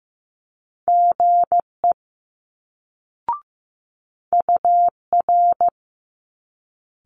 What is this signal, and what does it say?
Morse code, heard as a keyed tone, about 15 wpm, 700 Hz. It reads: GE UR
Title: Multi-letter Phonemes Quick Fire 15wpm